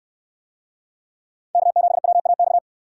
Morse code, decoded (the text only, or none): K7WAX